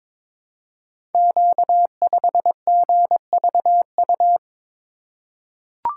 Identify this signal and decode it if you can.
Morse code, heard as a keyed tone, about 22 wpm, 700 Hz. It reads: Q5GVU
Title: Code Groups - 5 Letter:Numbers - 22wpm